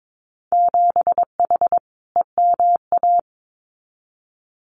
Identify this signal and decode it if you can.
Morse code, heard as a keyed tone, about 22 wpm, 700 Hz. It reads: 7H EMA
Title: Winter Field Day Exchanges Encore 22wpm